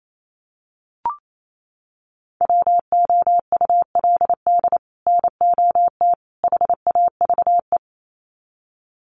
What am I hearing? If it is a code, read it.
Morse code, heard as a keyed tone, about 28 wpm, 700 Hz. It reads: WOULD NOT HAVE
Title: Sets of 3 Words 28wpm